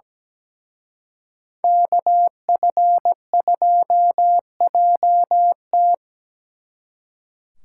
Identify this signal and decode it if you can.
Morse code, heard as a keyed tone, about 20 wpm, 700 Hz. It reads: KF2JT
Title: Speed Racing - USA 2x2 Call Signs 17wpm